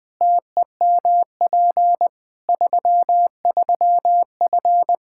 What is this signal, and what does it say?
Morse code, heard as a keyed tone, about 20 wpm, 700 Hz. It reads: TEMP 33F
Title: Practice QSO - Exchange 2 - Info - 20wpm